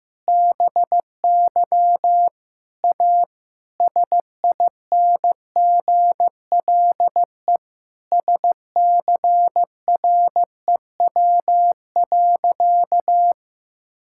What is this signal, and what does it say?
Morse code, heard as a keyed tone, about 15 wpm, 700 Hz. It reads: BY A SINGLE SCREW.
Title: Sentences from Top 1600 Words 15wpm